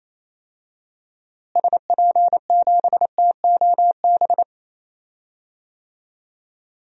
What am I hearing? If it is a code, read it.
Morse code, heard as a keyed tone, about 28 wpm, 700 Hz. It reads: SP7TOB
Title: DXCC-100 Call Signs - 6 Characters - Repeated 3x - 28wpm